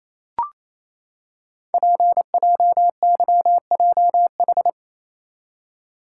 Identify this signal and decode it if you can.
Morse code, heard as a keyed tone, about 28 wpm, 700 Hz. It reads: PJYJH
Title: Code Groups - 5 Letters - 28wpm